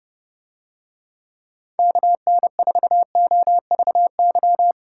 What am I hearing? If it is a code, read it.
Morse code, heard as a keyed tone, about 29 wpm, 700 Hz. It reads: KN4OVY